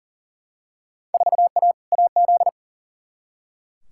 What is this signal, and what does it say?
Morse code, heard as a keyed tone, about 40 wpm, 700 Hz. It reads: 4A AZ
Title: Field Day Exchanges Encore - 40wpm